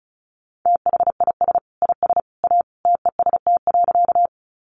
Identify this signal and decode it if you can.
Morse code, heard as a keyed tone, about 34 wpm, 700 Hz. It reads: THIS IS A TEST.
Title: Sentences from Top 500 Words 35wpm